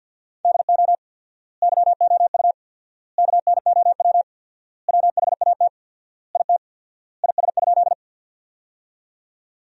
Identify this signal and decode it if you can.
Morse code, heard as a keyed tone, about 48 wpm, 700 Hz. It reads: DO YOU KNOW WHAT IT IS?
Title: 4x Word-Spacing - Sentences from Top 100 Words - 50wpm